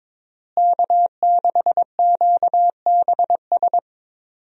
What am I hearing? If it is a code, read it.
Morse code, heard as a keyed tone, about 22 wpm, 700 Hz. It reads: K6QBS
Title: USA Call Signs - 5 Characters - Repeated 2x - 22wpm